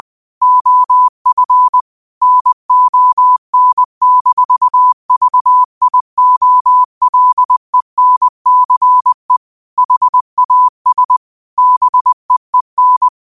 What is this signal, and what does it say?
Morse code, heard as a keyed tone, about 20 wpm, 1000 Hz. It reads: OF NON-VIOLENCE HAS BEEN
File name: qotdfiles / 20wpm